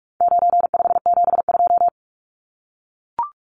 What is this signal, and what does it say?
Morse code, heard as a keyed tone, about 44 wpm, 700 Hz. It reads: 9572
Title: Numbers - 4-Digit 45wpm